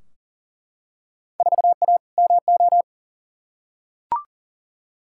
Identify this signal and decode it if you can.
Morse code, heard as a keyed tone, about 40 wpm, 700 Hz. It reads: 4A MO
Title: Field Day Exchanges - 40wpm